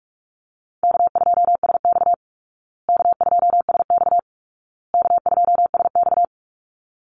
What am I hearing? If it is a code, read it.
Morse code, heard as a keyed tone, about 44 wpm, 700 Hz. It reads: K2SX K2SX K2SX
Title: Call Signs Repeated 2x - 45wpm